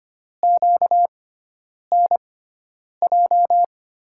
Morse code, heard as a keyed tone, about 25 wpm, spaced as for 9 wpm, 700 Hz. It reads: QNJ
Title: USA Call Signs - 6 Characters - 6x Character Spacing - 25wpm